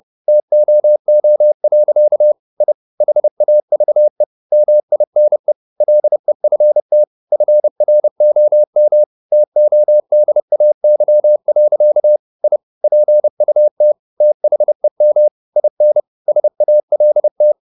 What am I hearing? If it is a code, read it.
Morse code, heard as a keyed tone, about 30 wpm, 600 Hz. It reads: TOO. I HAVE MINE LEFT FROM TODAY. I PUT THEM IN SALT
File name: the old man and the sea / the old man and the sea 1 30wpm